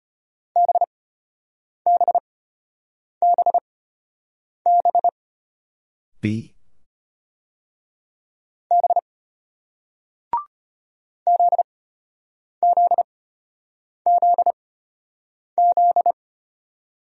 Morse code, heard as a keyed tone, about 34 wpm, spaced as for 8 wpm, 700 Hz. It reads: BBBB B ZZZZ